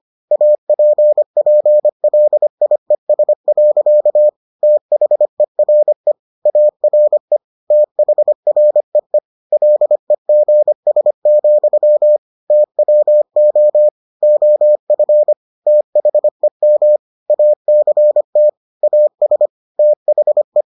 Morse code, heard as a keyed tone, about 25 wpm, 600 Hz. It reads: APPLIES. THERE ARE THREE LEGS, TWO OF THEM ACT AS THE